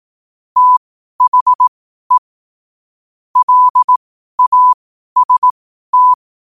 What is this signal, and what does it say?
Morse code, heard as a keyed tone, about 18 wpm, spaced as for 12 wpm, 1000 Hz. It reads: THE LAST
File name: qotdfiles / 12wpm